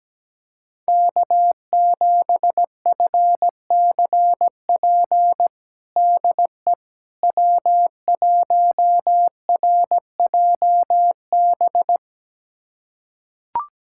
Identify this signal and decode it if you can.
Morse code, heard as a keyed tone, about 17 wpm, 700 Hz. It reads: K7FCP DE W1RJB
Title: USA Callsign DE Callsign - 5 Characters - 17wpm